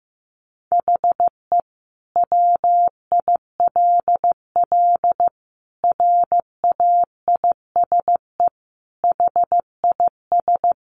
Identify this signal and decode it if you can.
Morse code, heard as a keyed tone, about 15 wpm, 700 Hz. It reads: HE WILL RAISE HIS
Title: Sentences from Top 1400 Words 15wpm